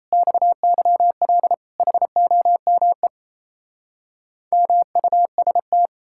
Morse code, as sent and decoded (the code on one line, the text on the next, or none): -..- -.-- .-.. / .... --- -- . / -- ..- ... -
XYL HOME MUST